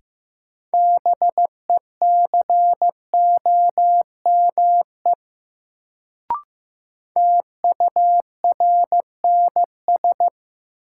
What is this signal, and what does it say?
Morse code, heard as a keyed tone, about 15 wpm, 700 Hz. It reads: BECOME TURNS
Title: Sets of 5 Words Encore - New Word Drill - 15wpm